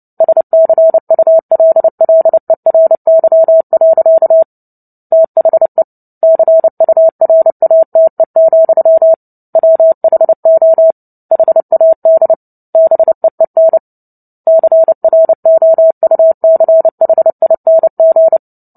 Morse code, heard as a keyed tone, about 30 wpm, 650 Hz. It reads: SCULLERY. THE CURATE, WHO HAD BEEN CROUCHING
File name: Book Two / 03-Chapter 3 - 29 WPM